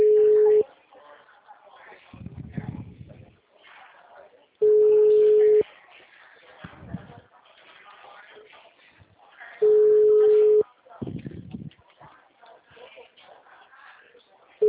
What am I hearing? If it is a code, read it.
Ringback tone (Europe, 425 Hz), with speech.